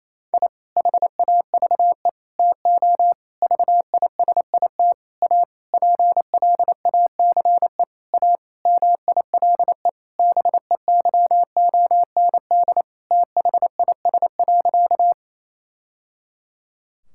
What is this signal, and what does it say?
Morse code, heard as a keyed tone, about 28 wpm, 700 Hz. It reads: I HAVE TO VISIT A PLACE A MILE BEYOND THIS.
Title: Sentences from Top 1000 Words - Review Part 6 - 28wpm